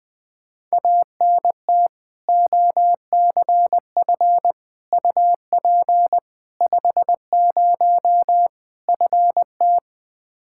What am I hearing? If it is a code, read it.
Morse code, heard as a keyed tone, about 20 wpm, 700 Hz. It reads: ANT OCF UP 50 FT
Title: Practice QSO - Exchange 2 - Info - 20wpm